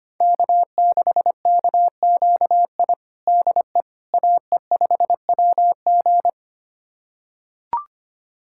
Morse code, heard as a keyed tone, about 25 wpm, 700 Hz. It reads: K6KQI DE AE5WG
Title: USA Callsign DE Callsign - 5 Characters - 25wpm